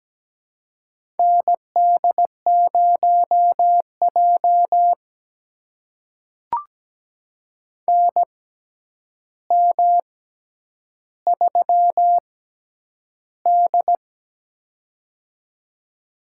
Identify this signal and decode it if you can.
Morse code, heard as a keyed tone, about 17 wpm, spaced as for 9 wpm, 700 Hz. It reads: ND0J NM3D
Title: USA Call Signs - 4 Characters - 6x Character Spacing - 17wpm